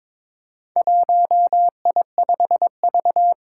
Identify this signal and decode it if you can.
Morse code, heard as a keyed tone, about 22 wpm, 700 Hz. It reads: 1I5V
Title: Four Challenging Letter-Number Combinations 22wpm